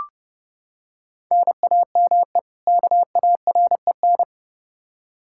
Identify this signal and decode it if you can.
Morse code, heard as a keyed tone, about 30 wpm, 700 Hz. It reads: NAME KAREN
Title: Practice QSO - Exchange 1 - Beginning - 30wpm